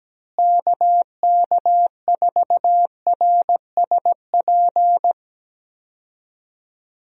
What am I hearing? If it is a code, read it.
Morse code, heard as a keyed tone, about 17 wpm, 700 Hz. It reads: KK4RSP
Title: USA Call Signs - 6 Characters - Repeated 2x - 17wpm